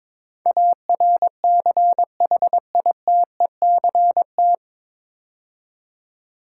Morse code, heard as a keyed tone, about 23 wpm, 700 Hz. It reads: ARCHITECT
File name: 9 Letter Words - First 3 Letters Given - 22wpm